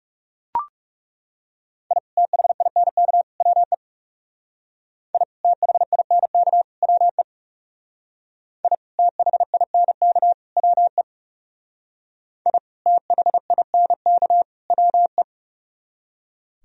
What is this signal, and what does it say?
Morse code, heard as a keyed tone, about 35 wpm, 700 Hz. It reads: I THINK WE I THINK WE I THINK WE I THINK WE
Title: Speed-Racing - Sets of 3 Words 30wpm